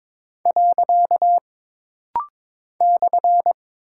Morse code, heard as a keyed tone, about 22 wpm, 700 Hz. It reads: . /
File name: practice / 22wpm / Single Punctuation-Prosign - Rapid-Fire - 22wpm